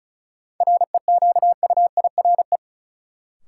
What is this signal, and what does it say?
Morse code, heard as a keyed tone, about 34 wpm, 700 Hz. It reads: REQUIRE
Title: Sentences from Top 900 Words - New Word Drill - 35wpm